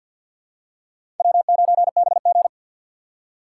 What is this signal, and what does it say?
Morse code, heard as a keyed tone, about 51 wpm, 700 Hz. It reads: W9BG
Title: USA Call Signs - 4 Characters - 3x Character Spacing - 50wpm